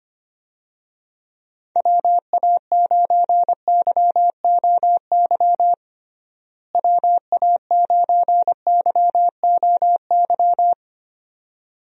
Morse code, heard as a keyed tone, about 25 wpm, 700 Hz. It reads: WA9YOY WA9YOY
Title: USA Call Signs - 6 Characters - Repeated 2x - 25wpm